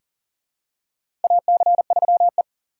Morse code, heard as a keyed tone, about 40 wpm, 700 Hz. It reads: AC3E
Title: USA Call Signs - 4 Characters - 6x Character Spacing - 40wpm